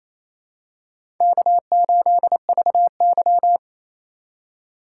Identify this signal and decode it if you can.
Morse code, heard as a keyed tone, about 28 wpm, 700 Hz. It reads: K8VY
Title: USA Call Signs - 4 Characters - 2x Character Spacing - 28wpm